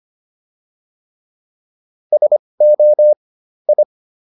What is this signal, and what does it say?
Morse code, heard as a keyed tone, about 25 wpm, spaced as for 20 wpm, 600 Hz. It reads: SO I